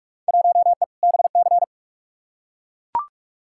Morse code, heard as a keyed tone, about 43 wpm, 700 Hz. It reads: 1E BC
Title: Field Day Exchanges - 45wpm